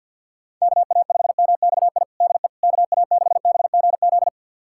Morse code, heard as a keyed tone, about 52 wpm, 700 Hz. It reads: KA5MXI DE KA6BGZ
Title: USA Callsign DE Callsign - 6 Characters - 50wpm